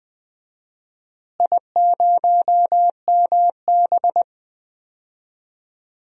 Morse code, heard as a keyed tone, about 20 wpm, 700 Hz. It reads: I0MB
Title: DXCC-100 Call Signs - 4 Characters - Repeated 3x - 20wpm